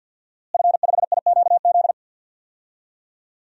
Morse code, heard as a keyed tone, about 51 wpm, 700 Hz. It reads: F5IQZ
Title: DXCC-100 Call Signs - 5 Characters - Repeated 2x - 50wpm